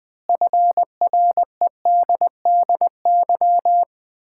Morse code, heard as a keyed tone, about 20 wpm, 700 Hz. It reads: FREDDY
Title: Common QSO Phrases - 3x Word-Spacing - 20wpm